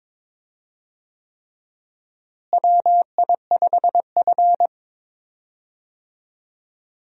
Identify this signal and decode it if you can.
Morse code, heard as a keyed tone, about 22 wpm, 700 Hz. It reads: WI5F
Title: USA Call Signs - 4 Characters - Repeated 3x - 22wpm